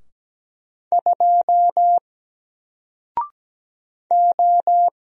Morse code, heard as a keyed tone, about 17 wpm, 700 Hz. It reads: 2 O
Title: Single Letter-Number 17wpm